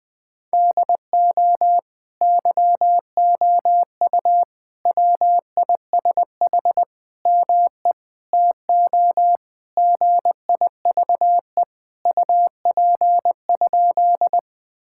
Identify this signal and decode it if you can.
Morse code, heard as a keyed tone, about 20 wpm, 700 Hz. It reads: DO YOU WISH ME TO GIVE UP?
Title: Sentences from Top 1600 Words 20wpm